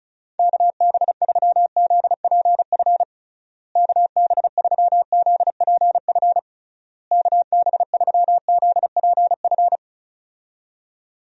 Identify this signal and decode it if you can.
Morse code, heard as a keyed tone, about 36 wpm, 700 Hz. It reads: KB3ZPF KB3ZPF KB3ZPF
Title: USA Call Signs - 6 Characters - Repeated 3x - 35wpm